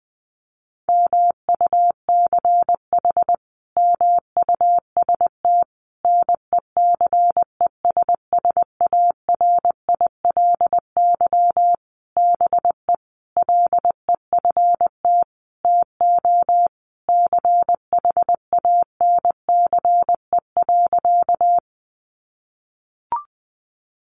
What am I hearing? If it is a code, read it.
Morse code, heard as a keyed tone, about 20 wpm, 700 Hz. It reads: MUCH MUST NECESSARILY BE LEFT TO CHANCE.
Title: Sentences from Top 1400 Words 20wpm